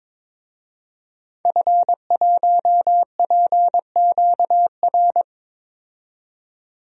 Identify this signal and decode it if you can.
Morse code, heard as a keyed tone, about 22 wpm, 700 Hz. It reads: F1PQR
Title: DXCC-100 Call Signs - 5 Characters - 22wpm